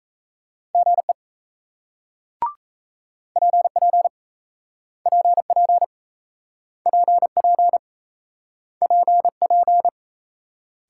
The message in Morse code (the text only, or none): GE PP PP PP PP